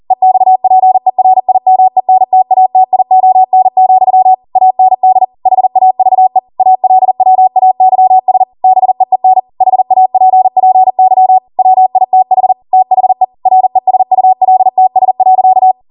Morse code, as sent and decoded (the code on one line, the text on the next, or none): . -..- .--. .-. .. -- . -. - .- - .. --- -. --..-- / .- -. -.. / .... .- ...- . / .- .-.. .-- .- -.-- ... / -... . -. / .... .- .--. .--. -.-- / .-- .. - .... / - .... / .-. ... ..- .-.. - ... .-.-.-
EXPERIMENTATION, AND HAVE ALWAYS BEEN HAPPY WITH THE RESULTS.